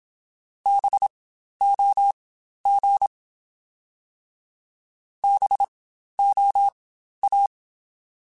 Morse code, heard as a keyed tone, about 27 wpm, spaced as for 12 wpm, 800 Hz. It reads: BOG BOA
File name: Spaced Alphas / Spaced Alphas mp3s / Bs sp @25